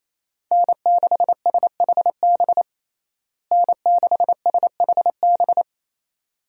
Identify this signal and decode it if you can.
Morse code, heard as a keyed tone, about 28 wpm, 700 Hz. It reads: N6SHB N6SHB
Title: USA Call Signs - 5 Characters - Repeated 2x - 28wpm